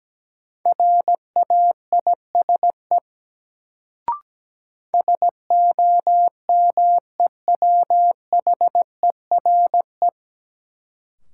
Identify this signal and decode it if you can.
Morse code, heard as a keyed tone, about 17 wpm, 700 Hz. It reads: RAISE SOMEWHERE
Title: Sentences from Top 1400 Words - New Word Drill Alt - 17wpm